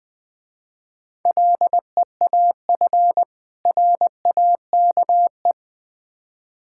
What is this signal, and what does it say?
Morse code, heard as a keyed tone, about 24 wpm, 700 Hz. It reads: LEAF RAKE
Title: Difficult Words - Speed-Racing with Hints - 20wpm